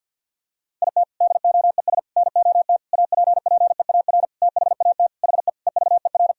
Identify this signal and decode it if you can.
Morse code, heard as a keyed tone, about 49 wpm, 700 Hz. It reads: IT DOES NOT APPEAR THAT HE EVER
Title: Sentences from Top 1000 Words - Review Part 2 - 50wpm